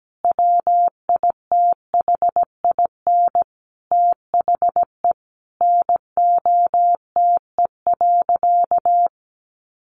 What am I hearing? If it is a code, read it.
Morse code, heard as a keyed tone, about 17 wpm, 700 Hz. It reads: WITHIN THE NOTE.
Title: Sentences from Top 1500 Words 17wpm